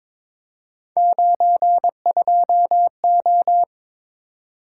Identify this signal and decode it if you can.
Morse code, heard as a keyed tone, about 23 wpm, 700 Hz. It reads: 92O
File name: practice / 22wpm / Three Letter-Number Combinations - 22wpm